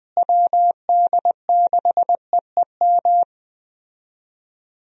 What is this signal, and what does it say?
Morse code, heard as a keyed tone, about 20 wpm, 700 Hz. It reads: WD6EEM
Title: USA Call Signs - 6 Characters - Repeated 2x - 20wpm